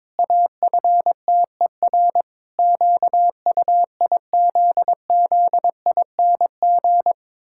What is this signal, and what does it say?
Morse code, heard as a keyed tone, about 22 wpm, 700 Hz. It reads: AFTER QUIZZING
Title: Pangram Challenge v2 - 22wpm